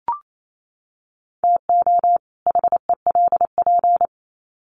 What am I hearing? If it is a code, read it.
Morse code, heard as a keyed tone, about 28 wpm, 700 Hz. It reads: TO HELP
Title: Sets of 2 Words 28wpm